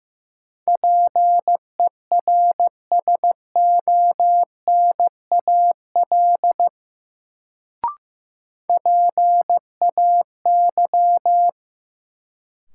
Morse code, heard as a keyed tone, about 15 wpm, 700 Hz. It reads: PERSONAL PAY
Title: Sentences from Top 700 Words - New Word Drill - 15wpm